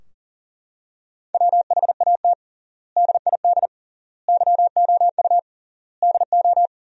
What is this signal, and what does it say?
Morse code, heard as a keyed tone, about 40 wpm, 700 Hz. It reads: WHAT DID YOU DO
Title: 3x Word-Spacing - Sentences from Top 100 Words - 40wpm